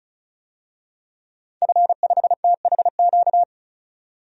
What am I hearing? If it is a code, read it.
Morse code, heard as a keyed tone, about 34 wpm, 700 Hz. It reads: F5THQ
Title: DXCC-100 Call Signs - 5 Characters - 35wpm